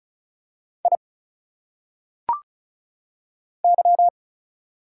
Morse code, heard as a keyed tone, about 36 wpm, 700 Hz. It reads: I Y
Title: Single Letters - Phonetic Pronunciation 35wpm